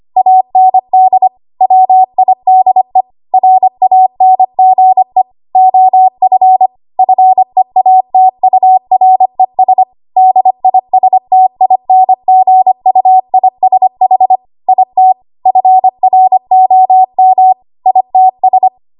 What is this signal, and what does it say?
Morse code, heard as a keyed tone, about 25 wpm, 750 Hz. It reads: AND WIDE RANGE OF FEATURES DISTINGUISH IT FROM ITS
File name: Archive / 25 WPM / 25WPM